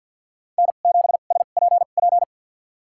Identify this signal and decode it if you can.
Morse code, heard as a keyed tone, about 52 wpm, spaced as for 37 wpm, 700 Hz. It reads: N7SPP
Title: USA Call Signs - 2x Character Spacing - 50wpm